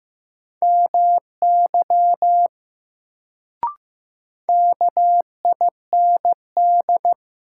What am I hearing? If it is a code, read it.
Morse code, heard as a keyed tone, about 15 wpm, 700 Hz. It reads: MY KIND
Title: Sentences from Top 200 Words - New Word Drill - 15wpm